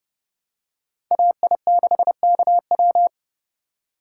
Morse code, heard as a keyed tone, about 28 wpm, 700 Hz. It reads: AI6KW